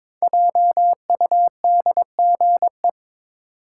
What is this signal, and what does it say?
Morse code, heard as a keyed tone, about 22 wpm, 700 Hz. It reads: JUDGE
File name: 5 Letter Words - First 3 Letters Given - 22wpm